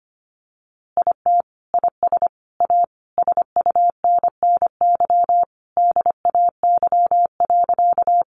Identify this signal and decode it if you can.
Morse code, heard as a keyed tone, about 25 wpm, 700 Hz. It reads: IT IS A SUNNY DAY.
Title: Easy Phrases 25wpm